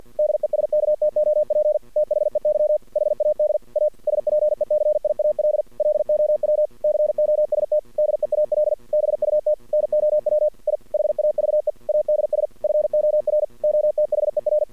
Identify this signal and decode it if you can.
Morse code, heard as a keyed tone, about 49 wpm, 600 Hz. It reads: DESKTOP THEY HAD A SPECIAL LOW COST DEAL BUT NOW I HAVE ADS FOR OTHER
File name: archived / CRSnet